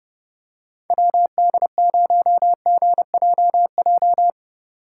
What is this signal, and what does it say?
Morse code, heard as a keyed tone, about 30 wpm, 700 Hz. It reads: WD0GJJ